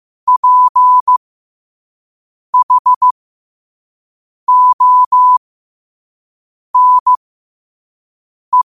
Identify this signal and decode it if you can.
Morse code, heard as a keyed tone, about 15 wpm, spaced as for 5 wpm, 1000 Hz. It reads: PHONE
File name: qotdfiles / 5wpm